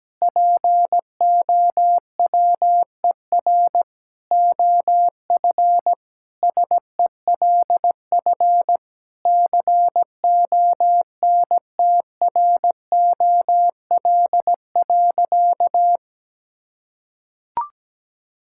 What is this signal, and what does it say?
Morse code, heard as a keyed tone, about 17 wpm, 700 Hz. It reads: POWER OF SELF CONTROL.
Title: Sentences from Top 1000 Words - Review Part 5 - 17wpm